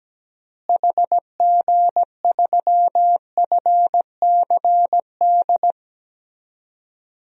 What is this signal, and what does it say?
Morse code, heard as a keyed tone, about 17 wpm, 700 Hz. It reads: HG3FCD
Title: DXCC-100 Call Signs - 6 Characters - Repeated 2x - 17wpm